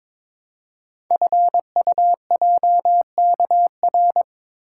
Morse code, heard as a keyed tone, about 23 wpm, 700 Hz. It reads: FUJKR
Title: Code Groups - 5 Letters - 22wpm